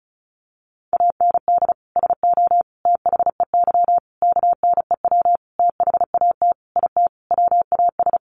Morse code, heard as a keyed tone, about 34 wpm, 700 Hz. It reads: AND SO THEY KNEW THAT IT WAS
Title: Sentences from Top 400 Words 35wpm